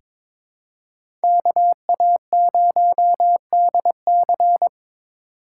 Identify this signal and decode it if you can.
Morse code, heard as a keyed tone, about 22 wpm, 700 Hz. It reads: KA0DC
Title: USA Call Signs - 2x Character Spacing - 22wpm